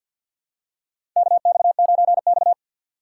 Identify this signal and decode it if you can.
Morse code, heard as a keyed tone, about 51 wpm, 700 Hz. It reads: KX9X
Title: USA Call Signs - 4 Characters - 3x Character Spacing - 50wpm